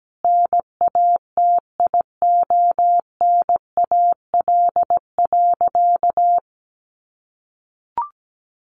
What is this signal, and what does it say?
Morse code, heard as a keyed tone, about 17 wpm, 700 Hz. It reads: NATIONAL.
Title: Sentences from Top 1200 Words 17wpm